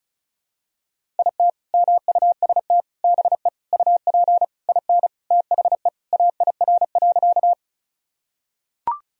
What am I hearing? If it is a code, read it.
Morse code, heard as a keyed tone, about 36 wpm, 700 Hz. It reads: IT MUST BE UP IN THE AIR.
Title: Sentences from Top 1000 Words - Review Part 1 - 35wpm